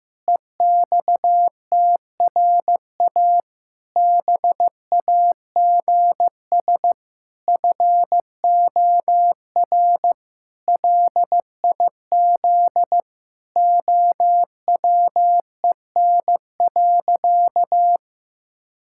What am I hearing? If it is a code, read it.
Morse code, heard as a keyed tone, about 16 wpm, 700 Hz. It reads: EXTRA BAGS FOR LIZ OWEN.
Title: Pangram Challenge v2 - 15wpm